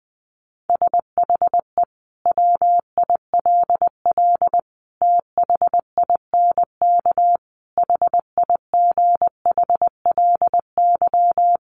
Morse code, heard as a keyed tone, about 20 wpm, 700 Hz. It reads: SHE WILL THINK HIGHLY
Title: Sentences from Top 1100 Words 20wpm